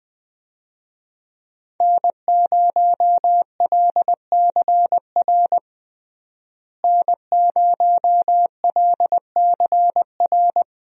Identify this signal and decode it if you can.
Morse code, heard as a keyed tone, about 21 wpm, 700 Hz. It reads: N0LCR N0LCR
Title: USA Call Signs - 5 Characters - Repeated 2x - 20wpm